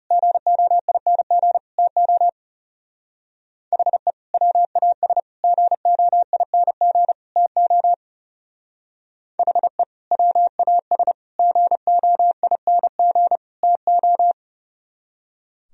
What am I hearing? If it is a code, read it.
Morse code, heard as a keyed tone, about 34 wpm, 700 Hz. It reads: GOING TO HE WAS GOING TO HE WAS GOING TO
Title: Speed-Racing - Sets of 4 Words 30wpm